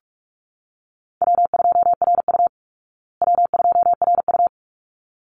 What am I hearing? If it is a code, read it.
Morse code, heard as a keyed tone, about 44 wpm, 700 Hz. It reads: W2RU W2RU
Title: Call Signs Repeated 3x - 45wpm